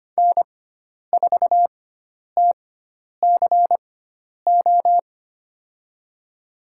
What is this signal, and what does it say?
Morse code, heard as a keyed tone, about 25 wpm, spaced as for 10 wpm, 700 Hz. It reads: N4TCO
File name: Lesson 16g - Letter c - Call Signs 25wpm